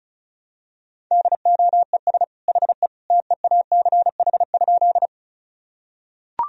Morse code, heard as a keyed tone, about 36 wpm, 700 Hz. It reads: DOES HE TEACH?
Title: Sentences from Top 1000 Words - Review Part 2 - 35wpm